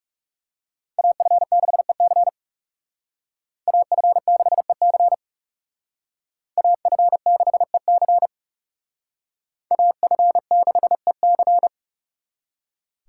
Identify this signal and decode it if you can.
Morse code, heard as a keyed tone, about 38 wpm, 700 Hz. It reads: AF6EC AF6EC AF6EC AF6EC